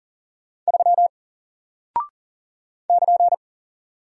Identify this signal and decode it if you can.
Morse code, heard as a keyed tone, about 40 wpm, 700 Hz. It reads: 3 <KN>
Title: Single Character (Letter:Number:Punctuation:Prosign) Alt 40wpm